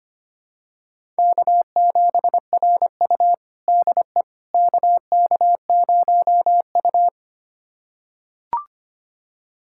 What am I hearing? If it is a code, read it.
Morse code, heard as a keyed tone, about 25 wpm, 700 Hz. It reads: K7RU DE KK0U
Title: USA Callsign DE Callsign - 4 Characters - 25wpm